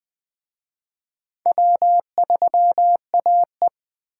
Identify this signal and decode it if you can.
Morse code, heard as a keyed tone, about 25 wpm, 700 Hz. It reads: W3AE
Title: Speed Racing - USA 1x2 Call Signs 20wpm